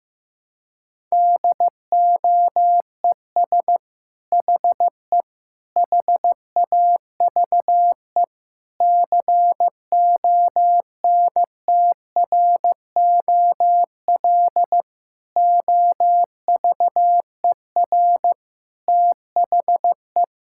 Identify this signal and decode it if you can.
Morse code, heard as a keyed tone, about 16 wpm, 700 Hz. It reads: DOES HE HAVE CONTROL OVER THE
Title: Sentences from Top 1000 Words - Review Part 5 - 15wpm